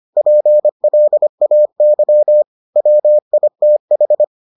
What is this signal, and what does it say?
Morse code, heard as a keyed tone, about 25 wpm, 600 Hz. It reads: PLAY WITH